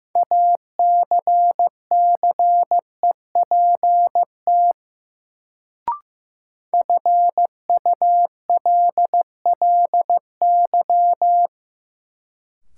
Morse code, heard as a keyed tone, about 15 wpm, 700 Hz. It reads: ACCEPT FULLY
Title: Sentences from Top 1300 Words - New Word Drill Alt - 15wpm